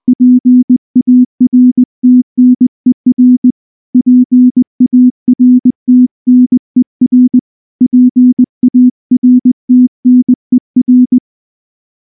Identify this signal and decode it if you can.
Morse code, heard as a keyed tone, about 19 wpm, 250 Hz. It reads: PARTNER PARTNER PARTNER
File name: morse / morse-training-20WPM-1